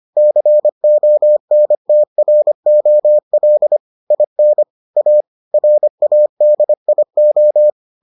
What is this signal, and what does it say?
Morse code, heard as a keyed tone, about 25 wpm, 600 Hz. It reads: CONTROL IN A RADIO